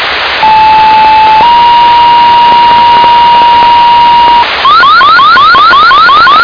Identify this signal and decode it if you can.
Two-tone page, heard as a one-second tone, then a three-second tone, about 870 then 950 Hz, with noise. It is heard from 0.4 to 4.4 s.